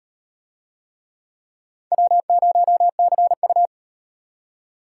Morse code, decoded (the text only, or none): W0CU